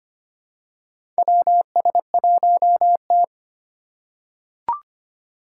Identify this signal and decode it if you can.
Morse code, heard as a keyed tone, about 25 wpm, 700 Hz. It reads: WS1T
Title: USA Call Signs - 4 Characters - 4x Character Spacing - 25wpm